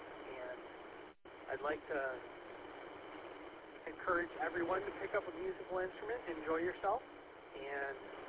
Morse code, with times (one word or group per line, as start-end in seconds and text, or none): none